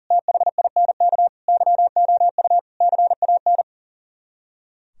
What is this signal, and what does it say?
Morse code, heard as a keyed tone, about 52 wpm, 700 Hz. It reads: THINK YOU CAN